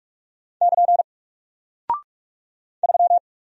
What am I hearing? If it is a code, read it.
Morse code, heard as a keyed tone, about 44 wpm, 700 Hz. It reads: <KN> 3
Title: Single Character (Letter:Number:Punctuation:Prosign) 45wpm